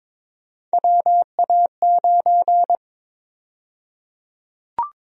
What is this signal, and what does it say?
Morse code, heard as a keyed tone, about 22 wpm, 700 Hz. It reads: WA9
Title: USA Call Sign Prefixes - 22wpm